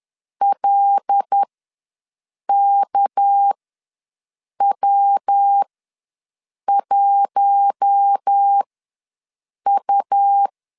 Morse code, heard as a keyed tone, about 10 wpm, spaced as for 6 wpm, 800 Hz. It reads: LKW1U
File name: LEZIONI / 6 PKX0 / 6 PK ws 10wpm 3745